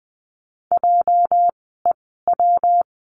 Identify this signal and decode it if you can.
Morse code, heard as a keyed tone, about 19 wpm, spaced as for 14 wpm, 700 Hz. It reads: JEW
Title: 206 Lesson 33f - Letter j - Words 20wpm